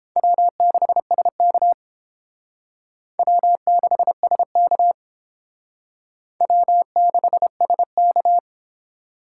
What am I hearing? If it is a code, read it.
Morse code, heard as a keyed tone, about 30 wpm, 700 Hz. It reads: W6SK W6SK W6SK